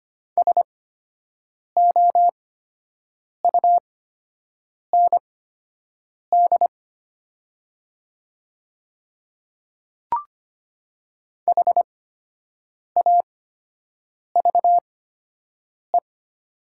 Morse code, heard as a keyed tone, about 25 wpm, spaced as for 7 wpm, 700 Hz. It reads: SOUND HAVE
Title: 8x Character Spacing - Top 100 Words - 25wpm